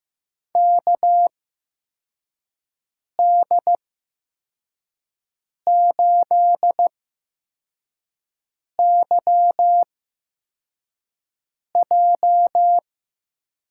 Morse code, heard as a keyed tone, about 15 wpm, spaced as for 4 wpm, 700 Hz. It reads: KD8YJ
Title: USA Call Signs - 8x Character Spacing - 15wpm